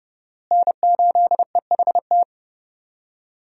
Morse code, heard as a keyed tone, about 30 wpm, 700 Hz. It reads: N8EHT
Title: USA Call Signs - 5 Characters - Repeated 3x - 30wpm